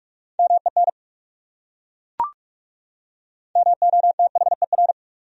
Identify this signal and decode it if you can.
Morse code, heard as a keyed tone, about 38 wpm, 700 Hz. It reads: MEN MOTHER